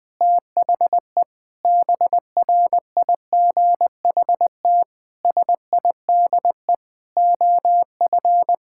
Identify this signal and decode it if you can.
Morse code, heard as a keyed tone, about 20 wpm, 700 Hz. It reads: THE BRIGHT SIDE OF
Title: Sentences with 8-Word Phrases - 20wpm